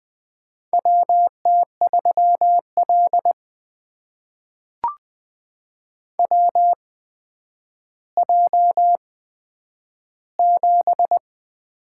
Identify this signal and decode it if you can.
Morse code, heard as a keyed tone, about 20 wpm, spaced as for 10 wpm, 700 Hz. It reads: WT3L WJ7
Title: USA Call Signs - 4 Characters - 8x Character Spacing - 20wpm